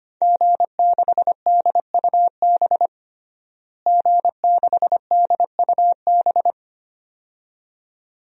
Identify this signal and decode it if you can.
Morse code, heard as a keyed tone, about 25 wpm, 700 Hz. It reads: G6DUB G6DUB
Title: DXCC-100 Call Signs - 5 Characters - Repeated 2x - 25wpm